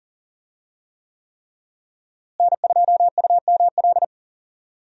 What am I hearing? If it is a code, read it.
Morse code, heard as a keyed tone, about 37 wpm, 700 Hz. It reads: N2UML
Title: USA Call Signs - 5 Characters - Repeated 3x - 40wpm